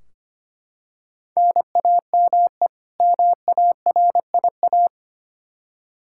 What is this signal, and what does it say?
Morse code, heard as a keyed tone, about 25 wpm, 700 Hz. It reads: NAME MARIA
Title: Practice QSO - Exchange 1 - Beginning - 25wpm